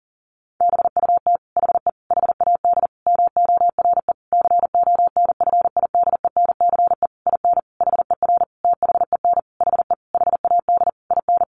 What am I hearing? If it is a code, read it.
Morse code, heard as a keyed tone, about 40 wpm, 700 Hz. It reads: BUT HE HAD MORE CONFIDENCE IN HER THEN HE HAD IN
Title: Sentences from Top 1500 Words 40wpm